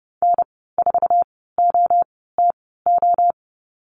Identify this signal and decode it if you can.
Morse code, heard as a keyed tone, about 30 wpm, spaced as for 17 wpm, 700 Hz. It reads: N4OTO
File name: Lesson 15g - Number 5 - Call Signs 30wpm